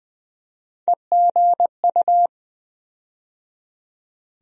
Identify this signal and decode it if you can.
Morse code, heard as a keyed tone, about 21 wpm, 700 Hz. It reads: EGU
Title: USA Call Sign Postfixes - 3 Letters 20wpm